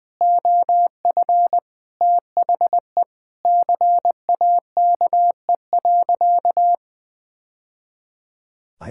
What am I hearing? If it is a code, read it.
Morse code, heard as a keyed tone, about 21 wpm, 700 Hz. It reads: OF THE CAKE.
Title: Sentences with 3-Word Phrases - 20wpm